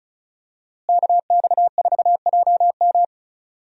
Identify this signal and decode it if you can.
Morse code, heard as a keyed tone, about 34 wpm, 700 Hz. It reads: KX4JM
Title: USA Call Signs - 5 Characters - Repeated 3x - 35wpm